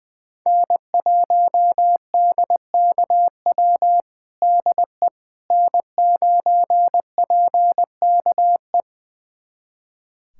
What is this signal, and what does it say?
Morse code, heard as a keyed tone, about 20 wpm, 700 Hz. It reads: N1DKW DE N9PKE
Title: USA Callsign DE Callsign - 5 Characters - 20wpm